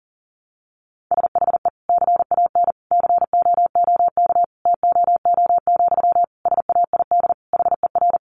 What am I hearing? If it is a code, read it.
Morse code, heard as a keyed tone, about 40 wpm, 700 Hz. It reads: SHE CAN COOK TOO, SAID HER
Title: Sentences from Top 700 Words 40wpm